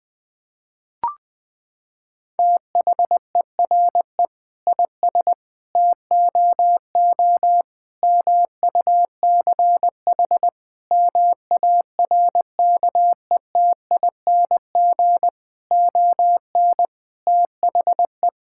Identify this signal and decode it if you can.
Morse code, heard as a keyed tone, about 20 wpm, 700 Hz. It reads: THERE IS TOO MUCH MARKETING ON THE
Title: Sentences from Top 1000 Words - Review Part 5 - 20wpm